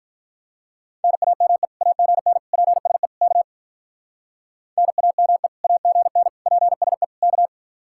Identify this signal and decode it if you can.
Morse code, heard as a keyed tone, about 44 wpm, 700 Hz. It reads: NAME AGN PSE K NAME AGN PSE K
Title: Speed-Racing Common QSO Phrases - 35wpm